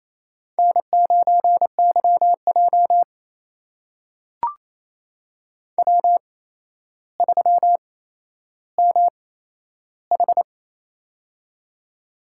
Morse code, heard as a keyed tone, about 27 wpm, spaced as for 12 wpm, 700 Hz. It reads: N9YJ W3MH
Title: USA Call Signs - 4 Characters - 8x Character Spacing - 28wpm